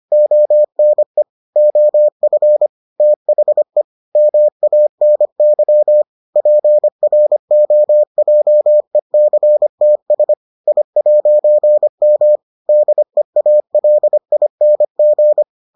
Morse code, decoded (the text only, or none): ONE OF THE MANY PROJECTS I'M DEALING